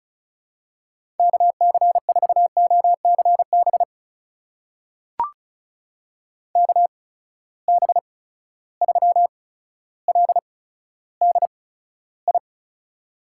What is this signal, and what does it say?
Morse code, heard as a keyed tone, about 36 wpm, spaced as for 15 wpm, 700 Hz. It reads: KC4OCB KB3LDI
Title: USA Call Signs - 6 Characters - 8x Character Spacing - 35wpm